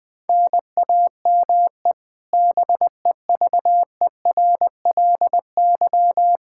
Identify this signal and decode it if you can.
Morse code, heard as a keyed tone, about 21 wpm, 700 Hz. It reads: NAME BEVERLY
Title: Practice QSO - Exchange 1 - Beginning - 20wpm